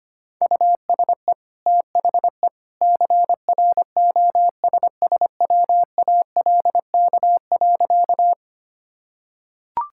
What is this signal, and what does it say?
Morse code, heard as a keyed tone, about 25 wpm, 700 Hz. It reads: USE THE CROSSWALK.
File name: Easy Sentences with Hints - 25wpm